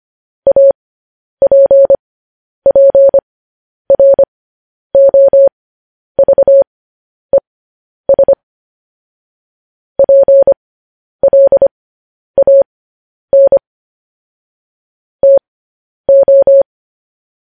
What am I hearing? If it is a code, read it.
Morse code, heard as a keyed tone, about 25 wpm, spaced as for 10 wpm, 550 Hz. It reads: APPROVES PLAN TO
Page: news-headlines-in-morse-code-at-10-wpm